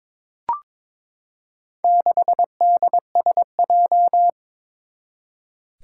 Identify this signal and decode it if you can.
Morse code, heard as a keyed tone, about 22 wpm, 700 Hz. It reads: 6DSJ
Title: Four Challenging Letter-Number Combinations 22wpm